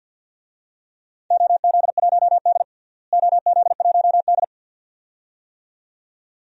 Morse code, heard as a keyed tone, about 52 wpm, 700 Hz. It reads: OZ1D OZ1D
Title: DXCC-100 Call Signs - 4 Characters - Repeated 2x - 50wpm